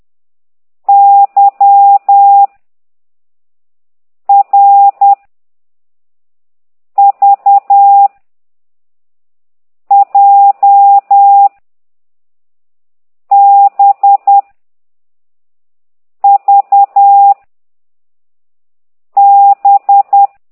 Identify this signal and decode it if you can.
Morse code, heard as a keyed tone, about 10 wpm, 800 Hz. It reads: Y R V J B V B